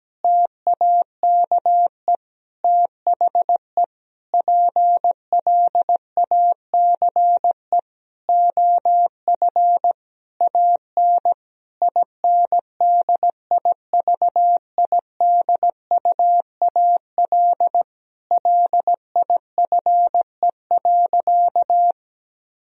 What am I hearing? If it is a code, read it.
Morse code, heard as a keyed tone, about 17 wpm, 700 Hz. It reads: TAKE THE PLACE OF AN INDIVIDUAL LIFE.
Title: Sentences from Top 1000 Words - Review Part 4 - 17wpm